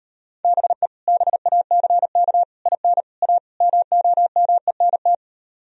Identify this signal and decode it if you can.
Morse code, heard as a keyed tone, about 38 wpm, 700 Hz. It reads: BE BACK IN A MOMENT